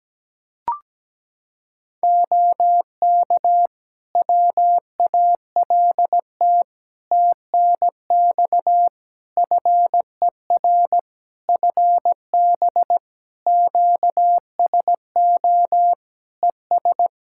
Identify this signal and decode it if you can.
Morse code, heard as a keyed tone, about 17 wpm, 700 Hz. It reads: OK WALT TNX FER FB QSO ES
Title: Practice QSO - Exchange 4 - Ending - 17wpm